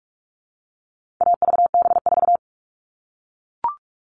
Morse code, heard as a keyed tone, about 44 wpm, 700 Hz. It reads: AVB4
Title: All Cut Numbers - 4-Digit 45wpm